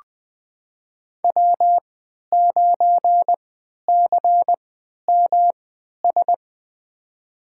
Morse code, heard as a keyed tone, about 20 wpm, spaced as for 11 wpm, 700 Hz. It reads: W9CMS
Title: USA Call Signs - 5 Characters - 3x Character Spacing - 20wpm